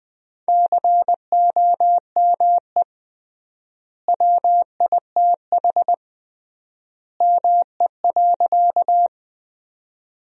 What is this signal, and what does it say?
Morse code, heard as a keyed tone, about 20 wpm, 700 Hz. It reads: COME WITH ME.
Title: 3x Word-Spacing - Sentences from Top 200 Words - 20wpm